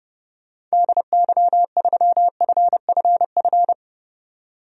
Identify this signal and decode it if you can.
Morse code, heard as a keyed tone, about 32 wpm, 700 Hz. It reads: DY3FFF
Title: DXCC-100 Call Signs - 6 Characters - 30wpm